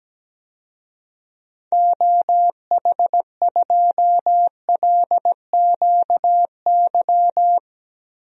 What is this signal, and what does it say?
Morse code, heard as a keyed tone, about 17 wpm, 700 Hz. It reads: OH2LQY
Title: DXCC-100 Call Signs - 6 Characters - 17wpm